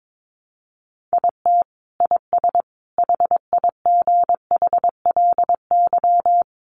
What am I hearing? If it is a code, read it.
Morse code, heard as a keyed tone, about 22 wpm, 700 Hz. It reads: IT IS HIGHLY
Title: Sentences from Top 1100 Words 22wpm